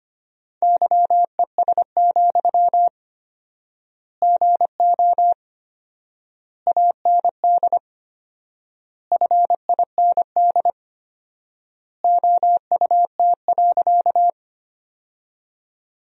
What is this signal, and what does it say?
Morse code, heard as a keyed tone, about 25 wpm, 700 Hz. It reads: YES, GO AND FIND OUT.
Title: 4x Word-Spacing - Sentences from Top 500 Words - 25wpm